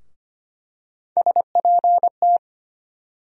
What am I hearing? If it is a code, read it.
Morse code, heard as a keyed tone, about 25 wpm, 700 Hz. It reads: SPT
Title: USA Call Sign Postfixes - 3 Letters 25wpm